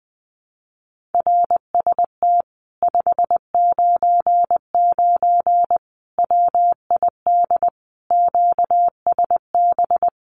Morse code, heard as a keyed tone, about 20 wpm, 700 Hz. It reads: RST 599 WID QSB
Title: Common QSO Phrases Encore 20wpm